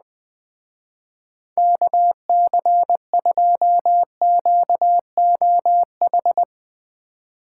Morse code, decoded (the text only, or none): KC2QOH